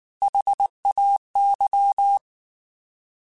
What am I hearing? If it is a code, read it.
Morse code, heard as a keyed tone, about 19 wpm, 800 Hz. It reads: HAY
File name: Alphabetical / Alphabetical mp3s / S vs H @18